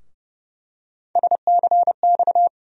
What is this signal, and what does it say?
Morse code, heard as a keyed tone, about 32 wpm, 700 Hz. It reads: SCX
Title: USA Call Sign Postfixes - 3 Letters 30wpm